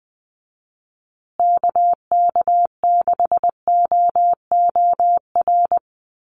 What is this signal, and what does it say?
Morse code, heard as a keyed tone, about 20 wpm, 700 Hz. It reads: KK6OOR
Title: USA Call Signs Repeated 3x 20wpm